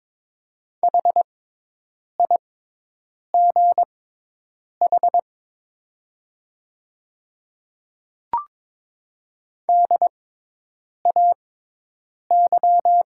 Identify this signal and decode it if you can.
Morse code, heard as a keyed tone, about 22 wpm, spaced as for 8 wpm, 700 Hz. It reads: HIGH DAY
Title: 6x Character Spacing - Top 200 Words - 22wpm